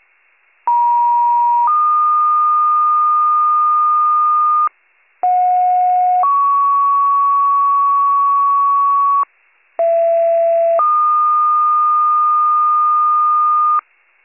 Two-tone page, heard as a one-second tone, then a three-second tone, about 950 then 1220 Hz, about 730 then 1120 Hz, about 670 then 1220 Hz. It is heard from 0.7 to 13.8 s.